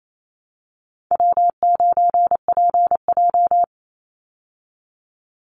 Morse code, heard as a keyed tone, about 28 wpm, 700 Hz. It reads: W9PJ
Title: Easy Call Signs Repeated 3x - 28wpm